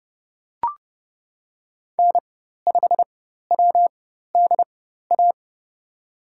Morse code, heard as a keyed tone, about 30 wpm, spaced as for 14 wpm, 700 Hz. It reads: N5WDA
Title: USA Call Signs - 5 Characters - 4x Character Spacing - 30wpm